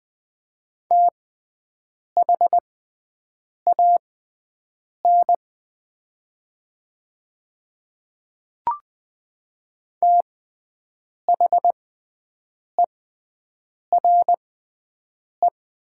Morse code, heard as a keyed tone, about 20 wpm, spaced as for 7 wpm, 700 Hz. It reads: THAN THERE
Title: 6x Character Spacing - Top 100 Words - 20wpm